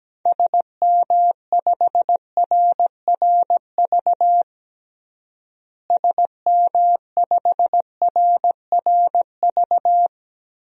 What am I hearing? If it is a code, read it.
Morse code, heard as a keyed tone, about 17 wpm, 700 Hz. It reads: SM5RRV SM5RRV
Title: DXCC-100 Call Signs - 6 Characters - Repeated 3x - 17wpm